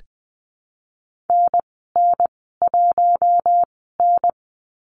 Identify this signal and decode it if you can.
Morse code, heard as a keyed tone, about 20 wpm, spaced as for 15 wpm, 700 Hz. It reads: NN1N
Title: 042 Lesson 8g - Number 1 - Call Signs 20wpm